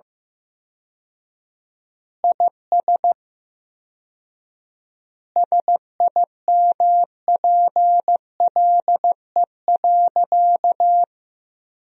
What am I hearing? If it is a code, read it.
Morse code, heard as a keyed tone, about 15 wpm, 700 Hz. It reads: IS SIMPLE.
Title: 4x Word-Spacing - Sentences from Top 400 Words - 15wpm